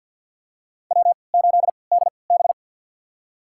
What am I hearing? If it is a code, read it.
Morse code, heard as a keyed tone, about 53 wpm, spaced as for 29 wpm, 700 Hz. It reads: W8DB
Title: USA Call Signs - 4 Characters - 3x Character Spacing - 50wpm